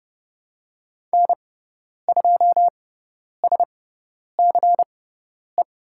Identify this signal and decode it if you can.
Morse code, heard as a keyed tone, about 30 wpm, spaced as for 10 wpm, 700 Hz. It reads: N2SCE
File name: Lesson 16g - Letter c - Call Signs 30wpm